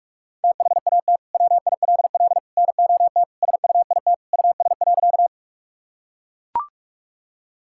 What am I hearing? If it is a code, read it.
Morse code, heard as a keyed tone, about 41 wpm, 700 Hz. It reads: THAT WILL NOT SUIT US.
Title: Sentences from Top 1600 Words 45wpm